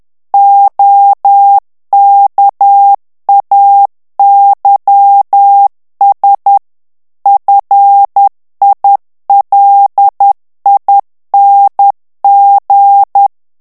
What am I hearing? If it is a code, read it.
Morse code, heard as a keyed tone, about 10 wpm, 800 Hz. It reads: OKAYS FILING
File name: ANGLAIS texte clair 3 vitesse 10 mots minute